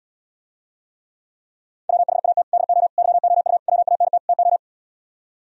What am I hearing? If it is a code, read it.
Morse code, heard as a keyed tone, about 78 wpm, 700 Hz. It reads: WHAT DO YOU WANT TO